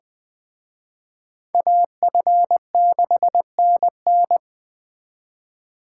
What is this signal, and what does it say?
Morse code, heard as a keyed tone, about 20 wpm, 700 Hz. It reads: AF6NN
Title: Speed Racing - USA Call Signs 17wpm